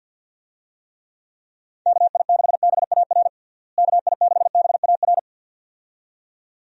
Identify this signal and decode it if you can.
Morse code, heard as a keyed tone, about 48 wpm, 700 Hz. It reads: KI6BAR KI6BAR
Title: USA Call Signs - 6 Characters - Repeated 2x - 50wpm